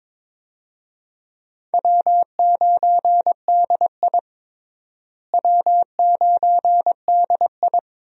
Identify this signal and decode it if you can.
Morse code, heard as a keyed tone, about 22 wpm, 700 Hz. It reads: W9DI W9DI
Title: USA Call Signs - 4 Characters - Repeated 3x - 22wpm